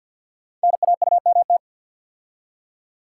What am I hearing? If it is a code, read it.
Morse code, heard as a keyed tone, about 48 wpm, 700 Hz. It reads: NAUMT
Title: Code Groups - 5 Letters - 50wpm